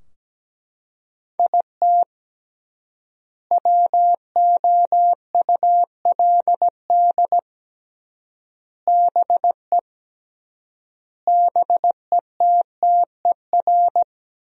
Morse code, heard as a keyed tone, about 17 wpm, 700 Hz. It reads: IT WOULD BE BETTER
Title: 3x Word-Spacing - Sentences from Top 400 Words - 17wpm